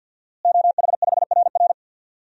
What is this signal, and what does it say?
Morse code, heard as a keyed tone, about 50 wpm, 700 Hz. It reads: OH5RR
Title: DXCC-100 Call Signs - 5 Characters - Repeated 3x - 50wpm